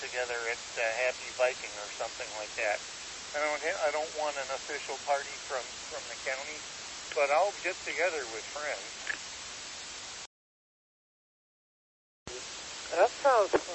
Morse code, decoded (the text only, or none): none